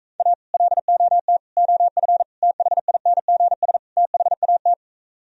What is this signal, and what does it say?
Morse code, heard as a keyed tone, about 44 wpm, 700 Hz. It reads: A LOT OF THINGS THAT